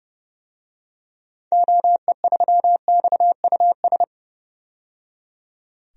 Morse code, heard as a keyed tone, about 29 wpm, 700 Hz. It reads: OE3XUS
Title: DXCC-100 Call Signs - 6 Characters - 30wpm